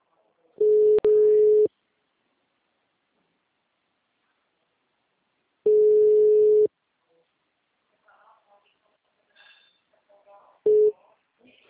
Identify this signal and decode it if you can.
Ringback tone (Europe, 425 Hz), with speech.